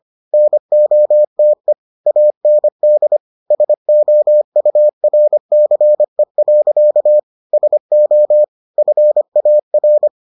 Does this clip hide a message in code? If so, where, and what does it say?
Yes: Morse code from 0.3 to 10.1 s, heard as a keyed tone, about 25 wpm, 600 Hz. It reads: NOTE AND SOURCE. SO FAR